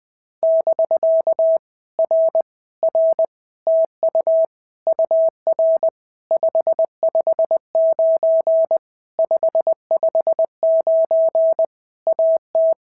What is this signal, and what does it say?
Morse code, heard as a keyed tone, about 20 wpm, 650 Hz. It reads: <BK> R R TU UR 559 559 AT
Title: POTA QSO Practice - 20wpm